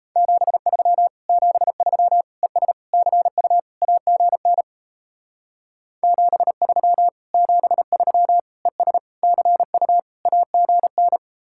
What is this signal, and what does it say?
Morse code, heard as a keyed tone, about 36 wpm, 700 Hz. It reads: 73 73 ES CU AGN 73 73 ES CU AGN